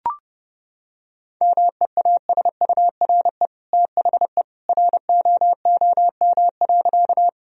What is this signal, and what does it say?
Morse code, heard as a keyed tone, about 30 wpm, 700 Hz. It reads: MEASURE THE ROOM.
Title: Sentences from Top 1000 Words - Review Part 2 - 30wpm